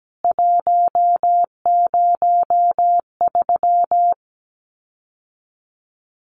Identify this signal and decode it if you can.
Morse code, heard as a keyed tone, about 17 wpm, 700 Hz. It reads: 103
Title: CWT Exchange 17wpm